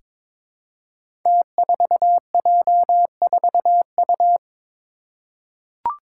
Morse code, heard as a keyed tone, about 22 wpm, 700 Hz. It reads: T4J4U
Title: Five Challenging Letter-Number Combinations 22wpm